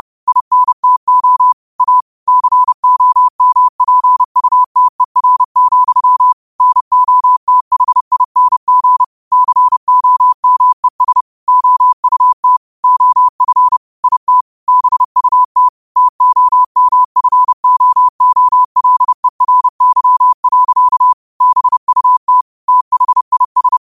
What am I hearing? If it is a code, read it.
Morse code, heard as a keyed tone, about 30 wpm, 1000 Hz. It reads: INTO A COMPUTER, NOTHING COMES OUT OF IT BUT TOMFOOLERY. BUT THIS
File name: qotdfiles / 30wpm